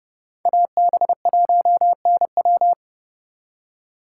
Morse code, heard as a keyed tone, about 39 wpm, 700 Hz. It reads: AB1NW